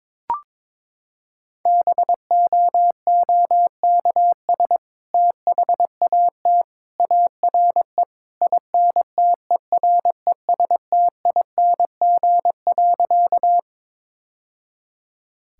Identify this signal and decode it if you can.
Morse code, heard as a keyed tone, about 22 wpm, 700 Hz. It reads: BOOKS THAT ARE INTERESTING.
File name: practice / 22wpm / Sentences with 2-Word Phrases - 22wpm